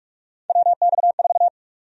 Morse code, heard as a keyed tone, about 47 wpm, 700 Hz. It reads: WX4
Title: USA Call Sign Prefixes - Repeated 2x - 45wpm